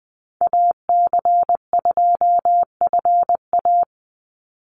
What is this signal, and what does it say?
Morse code, heard as a keyed tone, about 20 wpm, 700 Hz. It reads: AC2FA
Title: Call Signs - 20wpm